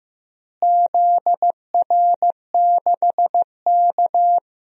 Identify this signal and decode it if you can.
Morse code, heard as a keyed tone, about 15 wpm, 700 Hz. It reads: ZR6K
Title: DXCC-100 Call Signs - 4 Characters - Repeated 3x - 15wpm